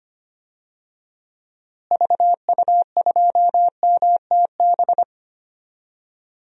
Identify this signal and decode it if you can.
Morse code, heard as a keyed tone, about 25 wpm, 700 Hz. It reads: VU2MTB
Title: DXCC-100 Call Signs - 6 Characters - Repeated 3x - 25wpm